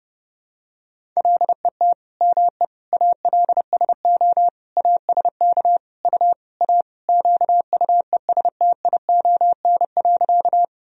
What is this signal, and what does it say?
Morse code, heard as a keyed tone, about 30 wpm, 700 Hz. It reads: LET ME ALSO ASK U A QUESTION.
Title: Sentences from Top 400 Words and Abbreviations 30wpm